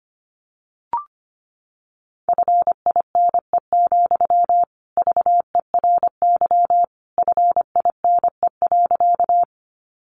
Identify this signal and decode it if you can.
Morse code, heard as a keyed tone, about 25 wpm, 700 Hz. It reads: FINE, VERY FINE.
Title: Sentences from Top 500 Words 25wpm